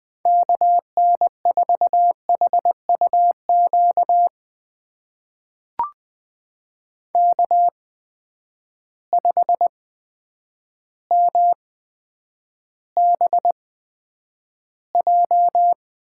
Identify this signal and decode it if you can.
Morse code, heard as a keyed tone, about 20 wpm, spaced as for 9 wpm, 700 Hz. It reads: KN4HUQ K5MBJ
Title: USA Call Signs - 8x Character Spacing - 20wpm